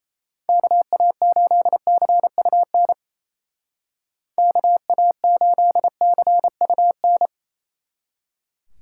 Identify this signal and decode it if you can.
Morse code, heard as a keyed tone, about 35 wpm, 700 Hz. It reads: KA8CUN KA8CUN
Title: Speed Racing - USA 2x3 Call Signs 28wpm